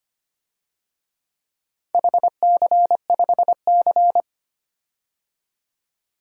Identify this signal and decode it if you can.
Morse code, heard as a keyed tone, about 25 wpm, 700 Hz. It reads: HC5C
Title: DXCC-100 Call Signs - 4 Characters - 25wpm